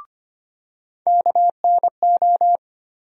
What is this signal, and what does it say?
Morse code, heard as a keyed tone, about 25 wpm, 700 Hz. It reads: KNO
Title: USA Call Sign Postfixes - 3 Letters 25wpm